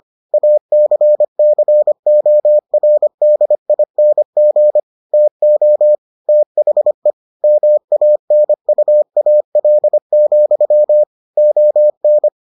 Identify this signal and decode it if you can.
Morse code, heard as a keyed tone, about 25 wpm, 600 Hz. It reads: ACCORDING TO THE MANUAL, ON